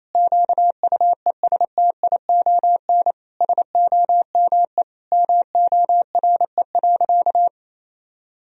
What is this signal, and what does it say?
Morse code, heard as a keyed tone, about 27 wpm, 700 Hz. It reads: QUESTION SOME MORE.
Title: Sentences from Top 1000 Words - Review Part 2 - 28wpm